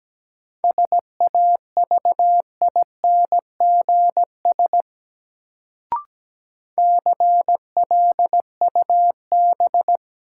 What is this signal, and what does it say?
Morse code, heard as a keyed tone, about 17 wpm, 700 Hz. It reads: SAVINGS CLUB
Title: Sentences from Top 1200 Words - New Word Drill Alt - 17wpm